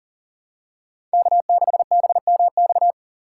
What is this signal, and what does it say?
Morse code, heard as a keyed tone, about 40 wpm, 700 Hz. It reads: K6BMX
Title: USA Call Signs - 2x Character Spacing - 40wpm